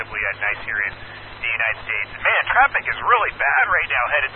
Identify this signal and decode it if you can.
Emergency Alert System (SAME) tones, heard as harsh buzzing bursts, with speech. It is heard from 0.4 to 3.8 s.